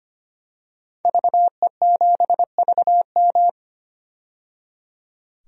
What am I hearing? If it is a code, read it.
Morse code, heard as a keyed tone, about 25 wpm, 700 Hz. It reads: VE7VM
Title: DXCC-100 Call Signs - 5 Characters - 25wpm